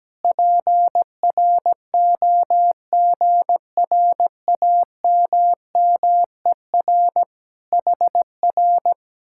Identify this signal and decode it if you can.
Morse code, heard as a keyed tone, about 17 wpm, 700 Hz. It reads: PROGRAMMER HR
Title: Speed-Racing Common QSO Phrases - 17wpm